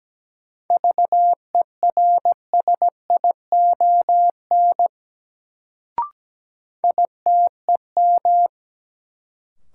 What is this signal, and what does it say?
Morse code, heard as a keyed tone, about 17 wpm, 700 Hz. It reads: VERSION ITEM
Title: Sentences from Top 1400 Words - New Word Drill Alt - 17wpm